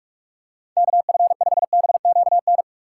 Morse code, heard as a keyed tone, about 44 wpm, 700 Hz. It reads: KF5BQN